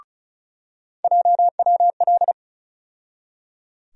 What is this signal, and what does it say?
Morse code, heard as a keyed tone, about 36 wpm, 700 Hz. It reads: JWL
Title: USA Call Sign Postfixes - 3 Letters 35wpm